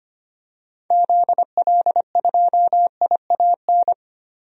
Morse code, heard as a keyed tone, about 25 wpm, 700 Hz. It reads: ZL2IAN
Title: DXCC-100 Call Signs - 6 Characters - Repeated 3x - 25wpm